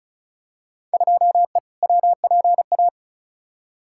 Morse code, heard as a keyed tone, about 34 wpm, 700 Hz. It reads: 2E WPA
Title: Field Day Exchanges - 35wpm